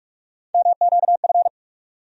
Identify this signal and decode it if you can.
Morse code, heard as a keyed tone, about 42 wpm, 700 Hz. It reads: MQF
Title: USA Call Sign Postfixes - 3 Letters 45wpm